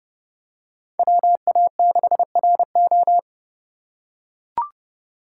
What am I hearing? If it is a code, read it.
Morse code, heard as a keyed tone, about 31 wpm, 700 Hz. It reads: WA6RO
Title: USA Call Signs - 3x Character Spacing - 30wpm